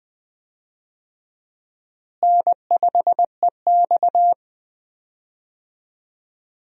Morse code, heard as a keyed tone, about 20 wpm, 700 Hz. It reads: N5EX
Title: USA Call Signs - 4 Characters - Repeated 2x - 20wpm